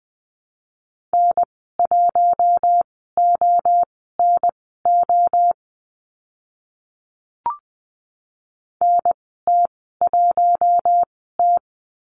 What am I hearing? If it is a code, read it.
Morse code, heard as a keyed tone, about 20 wpm, spaced as for 14 wpm, 700 Hz. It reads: N1ONO NT1T
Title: 042 Lesson 8g - Number 1 - Call Signs 20wpm